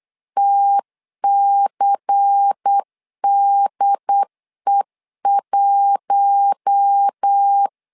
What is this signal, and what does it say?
Morse code, heard as a keyed tone, about 8 wpm, 800 Hz. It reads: TCDE1
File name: LEZIONI / 5 LBVC / 5 LBVC 8wpm 2935